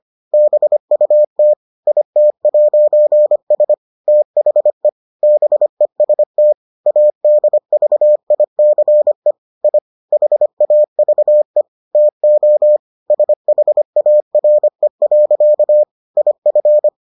Morse code, heard as a keyed tone, about 25 wpm, 600 Hz. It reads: BUT IT'S THE BEST ADVICE I HAVE TO SHARE. IF